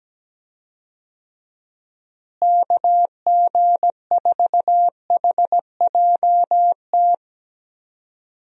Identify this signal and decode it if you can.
Morse code, heard as a keyed tone, about 17 wpm, 700 Hz. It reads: KG4HJT
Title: USA Call Signs - 6 Characters - Repeated 3x - 17wpm